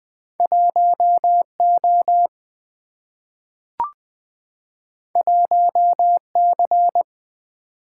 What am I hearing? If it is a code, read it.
Morse code, heard as a keyed tone, about 20 wpm, 700 Hz. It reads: 1O 1C
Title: Single-Letter Single-Number 20wpm